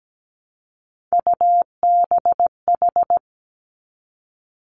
Morse code, heard as a keyed tone, about 17 wpm, 700 Hz. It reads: UBH
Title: Three Challenging Letter-Number Combinations 17wpm